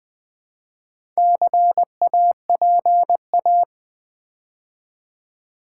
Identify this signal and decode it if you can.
Morse code, heard as a keyed tone, about 20 wpm, 700 Hz. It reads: CAPA
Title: Build A Word - 2 Letters At A Time - 8-Letter Words - 20wpm